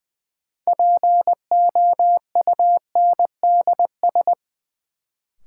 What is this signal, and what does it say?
Morse code, heard as a keyed tone, about 20 wpm, 700 Hz. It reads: POUNDS
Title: Sentences from Top 1000 Words - Review Part 2 - New Word Drill - 20wpm